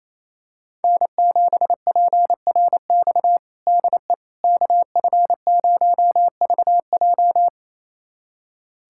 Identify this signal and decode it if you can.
Morse code, heard as a keyed tone, about 28 wpm, 700 Hz. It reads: N7PRX DE KF0VJ
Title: USA Callsign DE Callsign - 5 Characters - 28wpm